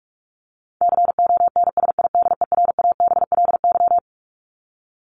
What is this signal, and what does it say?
Morse code, heard as a keyed tone, about 44 wpm, 700 Hz. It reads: CONSIDERABLY
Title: LY Suffix - Top 100 Words with Base Words 45wpm